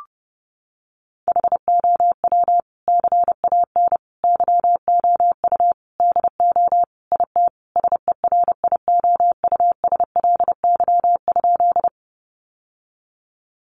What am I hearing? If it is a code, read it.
Morse code, heard as a keyed tone, about 28 wpm, 700 Hz. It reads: HOW CAN YOU DO IT SERIOUSLY?
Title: Sentences from Top 1200 Words 30wpm